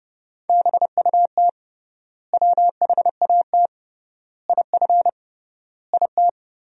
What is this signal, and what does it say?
Morse code, heard as a keyed tone, about 31 wpm, 700 Hz. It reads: BUT WHAT IF IT
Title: 3x Word-Spacing - Sentences from Top 100 Words - 30wpm